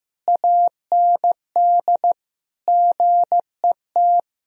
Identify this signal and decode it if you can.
Morse code, heard as a keyed tone, about 15 wpm, 700 Hz. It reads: AND GET
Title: Sentences from Top 1000 Words - Review Part 2 - 15wpm